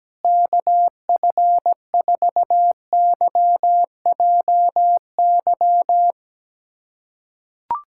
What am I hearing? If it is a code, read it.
Morse code, heard as a keyed tone, about 17 wpm, 700 Hz. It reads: KF4YJY
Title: USA Callsign DE Callsign - 6 Characters - 17wpm